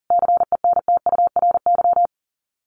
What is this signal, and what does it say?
Morse code, heard as a keyed tone, about 40 wpm, 700 Hz. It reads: CENTURY
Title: Sets of 5 Words Encore2 40wpm